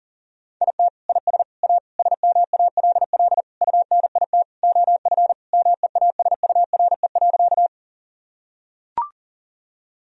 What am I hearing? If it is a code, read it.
Morse code, heard as a keyed tone, about 40 wpm, 700 Hz. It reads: IT IS A SMALL UNIT OF MEASURE.
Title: Sentences from Top 1000 Words - Review Part 2 - 40wpm